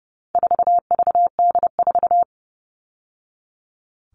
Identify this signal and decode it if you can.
Morse code, heard as a keyed tone, about 30 wpm, 700 Hz. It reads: 4VD4
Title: All Cut Numbers - 4-Digit 30wpm